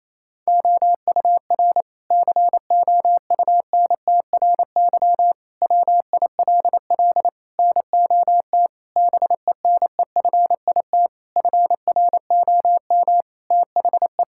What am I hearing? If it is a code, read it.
Morse code, heard as a keyed tone, about 28 wpm, 700 Hz. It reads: OUR COUNTRY WILL NOT BENEFIT FROM THE
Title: Sentences from Top 1000 Words - Review Part 6 - 28wpm